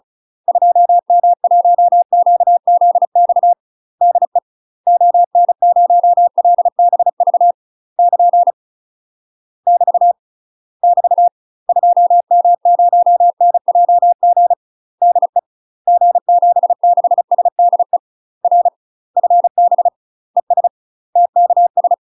Morse code, heard as a keyed tone, about 35 wpm, 700 Hz. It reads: 2M1QZX DE ON0LBV <KN> <BT> <BT> 2M0NJG DE G76SDE R FB ES TKS
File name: UK MASTER QSOs TRAINING / 35WPM / UK MASTER QSOs 0011